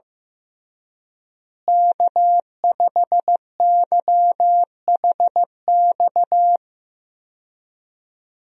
Morse code, heard as a keyed tone, about 15 wpm, 700 Hz. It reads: K5YHX
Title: USA Call Signs - 5 Characters - Repeated 3x - 15wpm